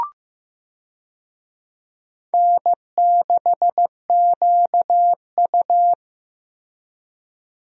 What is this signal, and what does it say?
Morse code, heard as a keyed tone, about 15 wpm, 700 Hz. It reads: N6QU
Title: USA Call Signs - 4 Characters - 15wpm